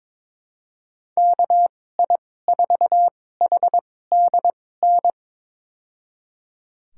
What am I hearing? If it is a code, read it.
Morse code, heard as a keyed tone, about 22 wpm, spaced as for 16 wpm, 700 Hz. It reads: KI4HDN
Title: USA Call Signs - 6 Characters - 2x Character Spacing - 22wpm